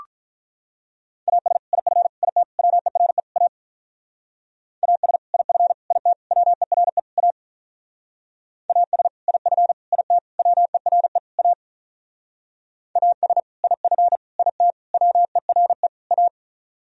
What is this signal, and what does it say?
Morse code, heard as a keyed tone, about 44 wpm, 700 Hz. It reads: AS IF IT WERE A AS IF IT WERE A AS IF IT WERE A AS IF IT WERE A